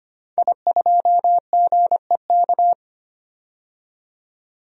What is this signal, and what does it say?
Morse code, heard as a keyed tone, about 25 wpm, 700 Hz. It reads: I2GEK
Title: DXCC-100 Call Signs - 5 Characters - 25wpm